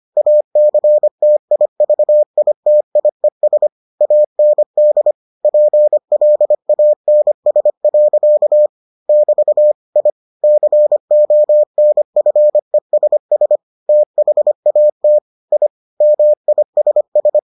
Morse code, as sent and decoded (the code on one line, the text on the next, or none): .- -.-. - .. ...- .. - .. ... / .- -. -.. / .--. .-.. .- -. ... .-.-.- / -...- / .. / -.-. --- -. ..-. ... ... / - .... .- - / .. / -- .. ... ...
ACTIVITIES AND PLANS. <BT> I CONFESS THAT I MISS